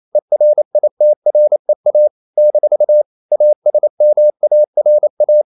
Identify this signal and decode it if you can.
Morse code, heard as a keyed tone, about 28 wpm, 600 Hz. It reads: ERITREA - ASMARA
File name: CW / Practice / NationalCapitals28